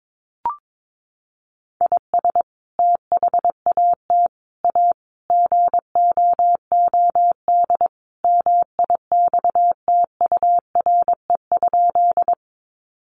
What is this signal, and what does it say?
Morse code, heard as a keyed tone, about 22 wpm, 700 Hz. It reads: IS THAT A GOOD MIXTURE?
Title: Sentences from Top 1500 Words 22wpm